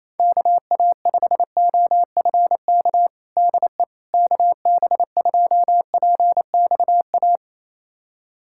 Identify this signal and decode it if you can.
Morse code, heard as a keyed tone, about 28 wpm, 700 Hz. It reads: KA5OFK DE KB2PXA
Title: USA Callsign DE Callsign - 6 Characters - 28wpm